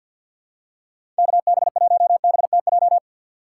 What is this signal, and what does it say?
Morse code, heard as a keyed tone, about 47 wpm, 700 Hz. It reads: KB1BTJ
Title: USA Call Signs - 6 Characters - 3x Character Spacing - 50wpm